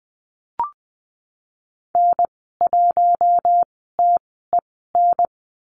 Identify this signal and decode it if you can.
Morse code, heard as a keyed tone, about 20 wpm, spaced as for 15 wpm, 700 Hz. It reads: N1TEN
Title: 055 Lesson 10g - Letter r - Call Signs 20wpm